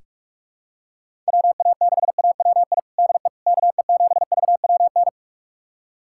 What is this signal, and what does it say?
Morse code, heard as a keyed tone, about 48 wpm, 700 Hz. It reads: WA6AWI DE KE7VWN
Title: USA Callsign DE Callsign - 6 Characters - 45wpm